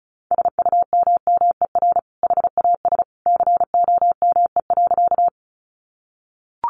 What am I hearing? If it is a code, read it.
Morse code, heard as a keyed tone, about 34 wpm, 700 Hz. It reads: SUMMER HAS COME.
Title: Sentences from Top 1100 Words 35wpm